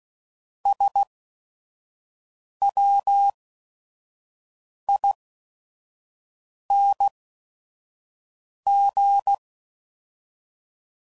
Morse code, heard as a keyed tone, about 16 wpm, spaced as for 5 wpm, 800 Hz. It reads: SWING